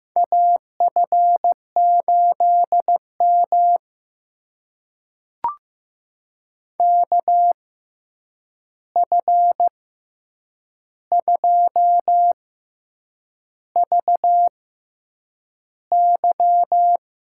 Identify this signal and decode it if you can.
Morse code, heard as a keyed tone, about 15 wpm, spaced as for 7 wpm, 700 Hz. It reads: AF8M KF2VY
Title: USA Call Signs - 6x Character Spacing - 15wpm